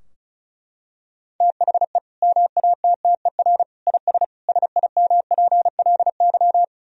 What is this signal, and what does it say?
Morse code, heard as a keyed tone, about 36 wpm, 700 Hz. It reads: THE MATTER IS SIMPLY
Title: Sentences from Top 1000 Words - Review Part 4 - 35wpm